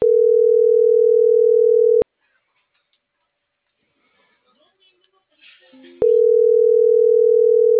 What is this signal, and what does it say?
Ringback tone (North America), 440 + 480 Hz.